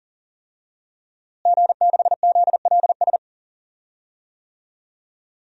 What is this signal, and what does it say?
Morse code, heard as a keyed tone, about 39 wpm, 700 Hz. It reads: G6ZLS
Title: DXCC-100 Call Signs - 5 Characters - Repeated 3x - 40wpm